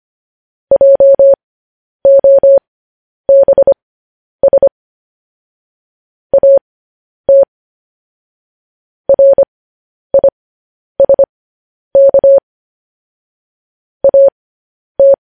Morse code, heard as a keyed tone, about 25 wpm, spaced as for 10 wpm, 550 Hz. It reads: JOBS AT RISK AT
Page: news-headlines-in-morse-code-at-10-wpm